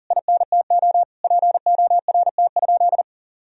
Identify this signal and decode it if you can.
Morse code, heard as a keyed tone, about 40 wpm, 700 Hz. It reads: INTO PORT?
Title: Sentences from Top 1000 Words - Review Part 1 - 40wpm